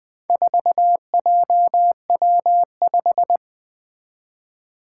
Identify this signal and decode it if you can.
Morse code, heard as a keyed tone, about 21 wpm, 700 Hz. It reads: 4JW5
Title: Four Challenging Letter-Number Combinations 20wpm